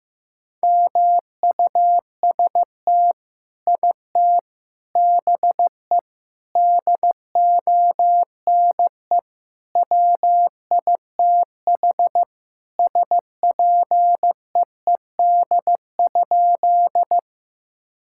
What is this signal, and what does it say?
Morse code, heard as a keyed tone, about 15 wpm, 700 Hz. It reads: MUST IT BE DONE WITH SPEED?
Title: Sentences from Top 1000 Words - Review Part 5 - 15wpm